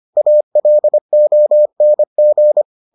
Morse code, heard as a keyed tone, about 25 wpm, 600 Hz. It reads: ALONG